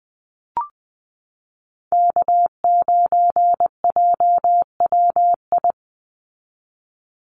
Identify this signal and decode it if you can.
Morse code, heard as a keyed tone, about 19 wpm, 700 Hz. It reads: K9JWI
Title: Call Signs Repeated 2x - 20wpm